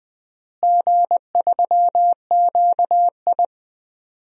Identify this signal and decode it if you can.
Morse code, heard as a keyed tone, about 20 wpm, 700 Hz. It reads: G3QI
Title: DXCC-100 Call Signs - 4 Characters - Repeated 3x - 20wpm